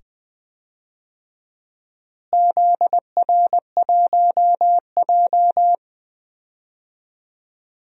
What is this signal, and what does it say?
Morse code, heard as a keyed tone, about 20 wpm, 700 Hz. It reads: ZR1J
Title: DXCC-100 Call Signs - 4 Characters - Repeated 2x - 20wpm